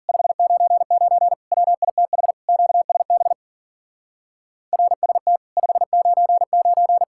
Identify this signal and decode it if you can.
Morse code, heard as a keyed tone, about 48 wpm, 700 Hz. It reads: 599 WITH QSB RST 599
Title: Speed-Racing Common QSO Phrases - 40wpm